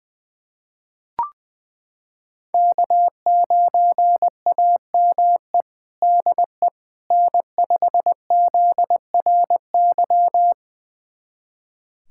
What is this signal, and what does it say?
Morse code, heard as a keyed tone, about 20 wpm, 700 Hz. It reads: K9AME DE N5ZRY
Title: USA Callsign DE Callsign - 5 Characters - 20wpm